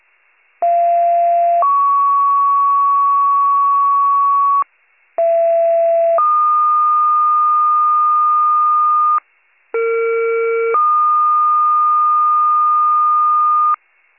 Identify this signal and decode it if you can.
Two-tone page, heard as a one-second tone, then a three-second tone, about 690 then 1120 Hz, about 670 then 1220 Hz, about 460 then 1220 Hz. It is heard from 0.6 to 13.8 s.